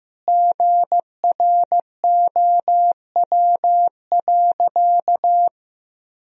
Morse code, heard as a keyed tone, about 15 wpm, 700 Hz. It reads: GROW.
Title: Sentences from Top 1000 Words - Review Part 2 - 15wpm